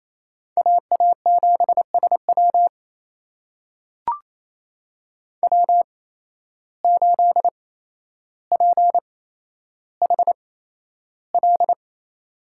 Morse code, heard as a keyed tone, about 28 wpm, spaced as for 12 wpm, 700 Hz. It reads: AA7SW W8PHL
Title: USA Call Signs - 5 Characters - 8x Character Spacing - 28wpm